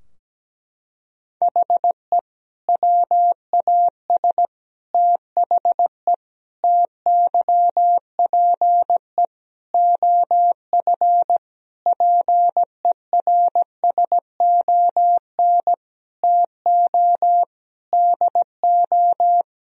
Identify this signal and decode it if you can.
Morse code, heard as a keyed tone, about 17 wpm, 700 Hz. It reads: HE WAS THE TYPE OF PERSON TO DO
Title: Sentences from Top 1000 Words - Review Part 3 - 17wpm